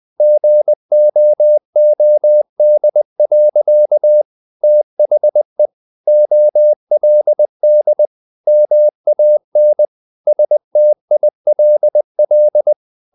Morse code, heard as a keyed tone, about 20 wpm, 600 Hz. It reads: GOOD. THE OLD MAN STILL
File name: the old man and the sea part 2 20wpm